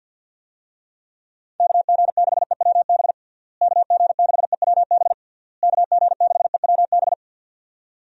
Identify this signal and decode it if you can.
Morse code, heard as a keyed tone, about 51 wpm, 700 Hz. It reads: KG6EWB KG6EWB KG6EWB
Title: USA Call Signs - 6 Characters - Repeated 3x - 50wpm